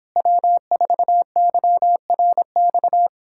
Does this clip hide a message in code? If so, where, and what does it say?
Yes: Morse code from 0.2 to 3.1 s, heard as a keyed tone, about 26 wpm, 700 Hz. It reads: W4YRX